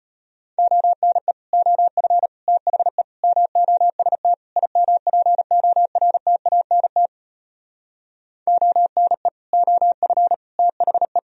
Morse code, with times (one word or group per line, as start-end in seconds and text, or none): 0.6-1.3 ONE
1.5-2.3 OF
2.5-3.0 THE
3.2-4.3 MOST
4.6-7.1 IMPORTANT
8.5-9.3 ONE
9.5-10.3 OF
10.6-11.2 THE